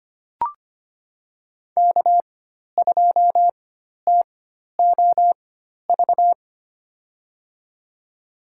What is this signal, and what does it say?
Morse code, heard as a keyed tone, about 25 wpm, spaced as for 12 wpm, 700 Hz. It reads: K2TOV
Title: USA Call Signs - 5 Characters - 4x Character Spacing - 25wpm